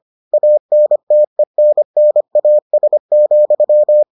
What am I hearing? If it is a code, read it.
Morse code, heard as a keyed tone, about 25 wpm, 600 Hz. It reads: ANTENNAS,